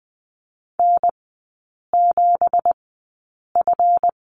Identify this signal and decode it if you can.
Morse code, heard as a keyed tone, about 20 wpm, spaced as for 8 wpm, 700 Hz. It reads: N7F
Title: 166 Lesson 27e - Number 7 - Three Characters 20wpm